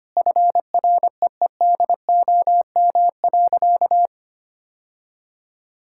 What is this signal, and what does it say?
Morse code, heard as a keyed tone, about 25 wpm, 700 Hz. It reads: FREEDOM.
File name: practice / 25wpm / Sentences with 3-Word Phrases - 25wpm